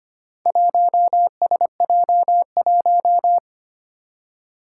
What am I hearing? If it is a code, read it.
Morse code, heard as a keyed tone, about 25 wpm, 700 Hz. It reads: 1SJ1
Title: Four Challenging Letter-Number Combinations 25wpm